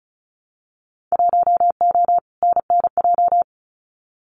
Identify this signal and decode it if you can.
Morse code, heard as a keyed tone, about 36 wpm, 700 Hz. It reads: 1O NNJ
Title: Winter Field Day Exchanges 35wpm